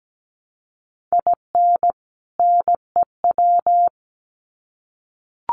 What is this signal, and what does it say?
Morse code, heard as a keyed tone, about 17 wpm, 700 Hz. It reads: IN NEW
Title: Sets of 2 Words 17wpm